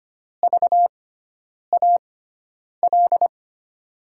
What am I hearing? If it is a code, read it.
Morse code, heard as a keyed tone, about 25 wpm, spaced as for 9 wpm, 700 Hz. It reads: VAL
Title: USA Call Signs - 6 Characters - 6x Character Spacing - 25wpm